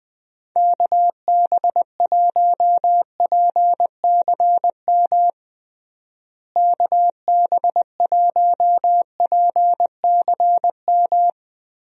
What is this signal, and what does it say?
Morse code, heard as a keyed tone, about 21 wpm, 700 Hz. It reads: KB1PCM KB1PCM
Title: USA Call Signs - 6 Characters - Repeated 3x - 20wpm